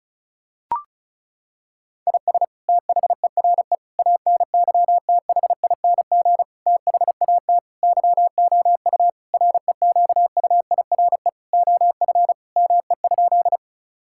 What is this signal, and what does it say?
Morse code, heard as a keyed tone, about 36 wpm, 700 Hz. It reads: IS THERE ANYTHING THAT YOU REQUIRE OF ME?
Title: Sentences from Top 1000 Words - Review Part 6 - 35wpm